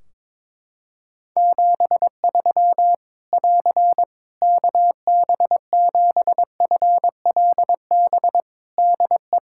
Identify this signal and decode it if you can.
Morse code, heard as a keyed tone, about 22 wpm, 700 Hz. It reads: 73 <AR> KB7FLB DE
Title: Practice QSO - Exchange 4 - Ending - 22wpm